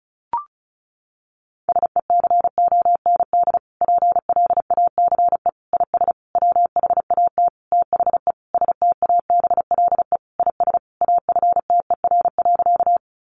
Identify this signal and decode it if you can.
Morse code, heard as a keyed tone, about 34 wpm, 700 Hz. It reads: SECOND PLACE IS WHAT THE STABLE IS AFTER.
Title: Sentences from Top 1500 Words 35wpm